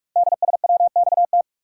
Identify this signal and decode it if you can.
Morse code, heard as a keyed tone, about 44 wpm, 700 Hz. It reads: DSWXT
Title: Code Groups - 5 Letters - 45wpm